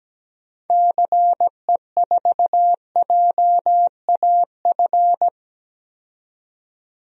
Morse code, heard as a keyed tone, about 17 wpm, 700 Hz. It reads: CE4JAF
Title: DXCC-100 Call Signs - 6 Characters - 17wpm